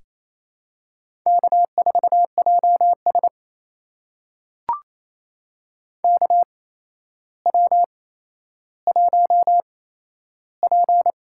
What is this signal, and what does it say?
Morse code, heard as a keyed tone, about 28 wpm, spaced as for 12 wpm, 700 Hz. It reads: K4JS KW1P
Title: USA Call Signs - 4 Characters - 8x Character Spacing - 28wpm